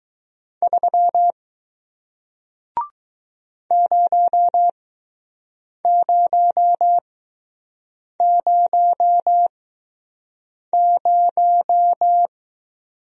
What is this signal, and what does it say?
Morse code, heard as a keyed tone, about 20 wpm, spaced as for 6 wpm, 700 Hz. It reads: 3 0000